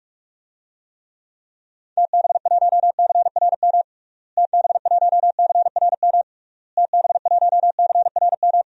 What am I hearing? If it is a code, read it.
Morse code, heard as a keyed tone, about 41 wpm, 700 Hz. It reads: TB1CRM TB1CRM TB1CRM
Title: DXCC-100 Call Signs - 6 Characters - Repeated 3x - 45wpm